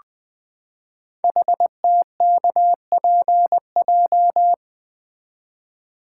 Morse code, heard as a keyed tone, about 20 wpm, 700 Hz. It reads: HTKPJ
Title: Code Groups - 5 Letter:Numbers - 20wpm